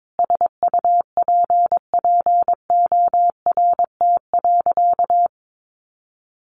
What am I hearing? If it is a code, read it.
Morse code, heard as a keyed tone, about 22 wpm, 700 Hz. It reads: SUPPORT.
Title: Sentences from Top 1300 Words 22wpm